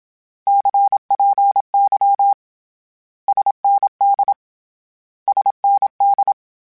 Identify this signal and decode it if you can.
Morse code, heard as a keyed tone, about 27 wpm, 800 Hz. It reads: CPY SND SND
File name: Abreviations / Abreviations mp3s / Abreviations@25